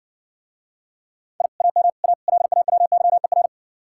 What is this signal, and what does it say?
Morse code, heard as a keyed tone, about 58 wpm, 700 Hz. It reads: I AM A LAWYER